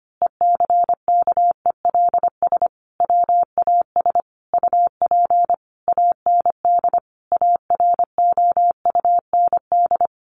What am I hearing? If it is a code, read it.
Morse code, heard as a keyed tone, about 25 wpm, 700 Hz. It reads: ECKELS WAS UP AND AROUND
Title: A Sound Of Thunder 25wpm